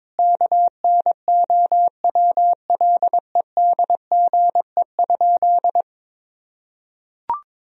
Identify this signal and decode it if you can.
Morse code, heard as a keyed tone, about 22 wpm, 700 Hz. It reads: KNOWLEDGE?
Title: Sentences from Top 1000 Words - Review Part 3 - 22wpm